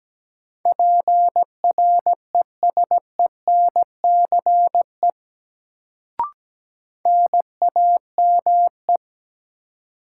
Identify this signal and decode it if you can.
Morse code, heard as a keyed tone, about 17 wpm, 700 Hz. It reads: PRESENCE NAME
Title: Sets of 3 Words - New Word Drill - 17wpm